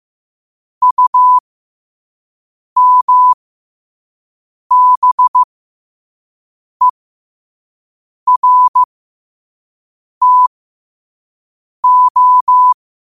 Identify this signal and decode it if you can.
Morse code, heard as a keyed tone, about 15 wpm, spaced as for 5 wpm, 1000 Hz. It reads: UMBERTO